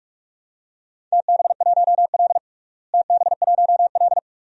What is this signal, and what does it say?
Morse code, heard as a keyed tone, about 46 wpm, 700 Hz. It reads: TB1L TB1L
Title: DXCC-100 Call Signs - 4 Characters - Repeated 2x - 45wpm